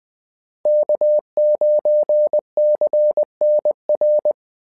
Morse code, heard as a keyed tone, about 20 wpm, 600 Hz. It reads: K9CNR
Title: SOTA QSO Practice 20wpm